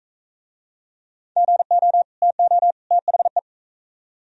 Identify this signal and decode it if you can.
Morse code, heard as a keyed tone, about 39 wpm, 700 Hz. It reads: GO TO THE